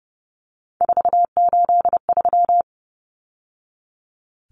Morse code, heard as a keyed tone, about 30 wpm, 700 Hz. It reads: 483
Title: Numbers - 3-Digit 30wpm